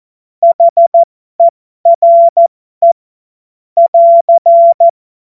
Morse code, heard as a keyed tone, about 14 wpm, spaced as for 12 wpm, 680 Hz. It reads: HERE <AR>
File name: Morsekurs / Witze